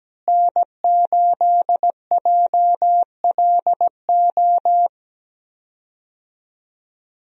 Morse code, heard as a keyed tone, about 17 wpm, 700 Hz. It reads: N8JLO
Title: USA Call Signs - 5 Characters - Repeated 3x - 17wpm